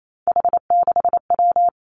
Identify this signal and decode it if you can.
Morse code, heard as a keyed tone, about 28 wpm, 690 Hz. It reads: H6W
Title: Three Challenging Letter-Number Combinations 28wpm